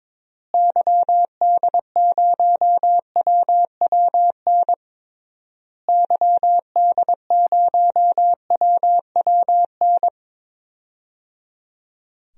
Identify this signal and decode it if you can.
Morse code, heard as a keyed tone, about 22 wpm, 700 Hz. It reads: YD0WWN YD0WWN
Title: DXCC-100 Call Signs - 6 Characters - Repeated 3x - 22wpm